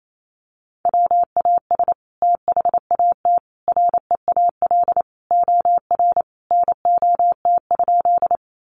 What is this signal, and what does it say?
Morse code, heard as a keyed tone, about 28 wpm, 700 Hz. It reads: WAS THAT REAL OR NOT?
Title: Sentences from Top 600 Words 28wpm